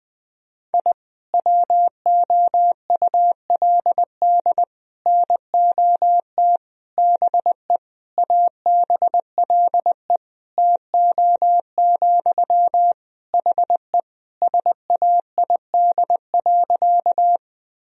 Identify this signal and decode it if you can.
Morse code, heard as a keyed tone, about 20 wpm, 700 Hz. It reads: I WOULD NOT BE ABLE TO, HE SAID.
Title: Sentences from Top 1000 Words - Review Part 2 - 20wpm